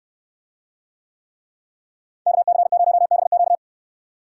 Morse code, heard as a keyed tone, about 66 wpm, 700 Hz. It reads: KC0BQ